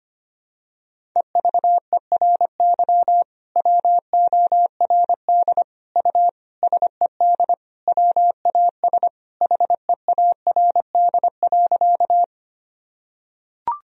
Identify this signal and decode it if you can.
Morse code, heard as a keyed tone, about 25 wpm, 700 Hz. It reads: EVERY WORD U SED WAS HEARD.
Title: Sentences from Top 400 Words and Abbreviations 25wpm